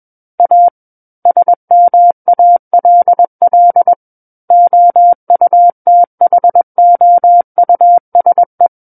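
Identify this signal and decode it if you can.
Morse code, heard as a keyed tone, about 21 wpm, 700 Hz. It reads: A SMALL OUTHOUSE
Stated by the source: Morse code practice reading of a book chapter at 21 words per minute.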